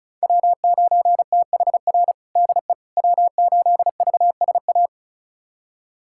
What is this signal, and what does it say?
Morse code, heard as a keyed tone, about 36 wpm, 700 Hz. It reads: W9THR DE W8VSA
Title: USA Callsign DE Callsign - 5 Characters - 35wpm